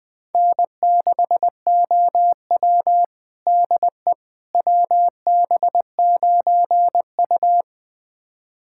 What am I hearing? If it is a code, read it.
Morse code, heard as a keyed tone, about 20 wpm, 700 Hz. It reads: N6OW DE WB9U
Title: USA Callsign DE Callsign - 4 Characters - 20wpm